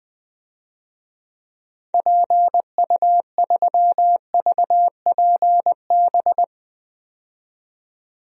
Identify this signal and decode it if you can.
Morse code, heard as a keyed tone, about 20 wpm, 700 Hz. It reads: PU3VPB
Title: DXCC-100 Call Signs - 6 Characters - 20wpm